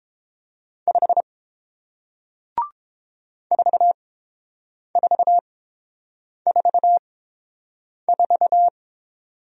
Morse code, heard as a keyed tone, about 31 wpm, spaced as for 7 wpm, 700 Hz. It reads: 5 4444